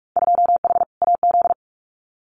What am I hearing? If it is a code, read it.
Morse code, heard as a keyed tone, about 46 wpm, 700 Hz. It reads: 2H AZ
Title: Winter Field Day Exchanges 45wpm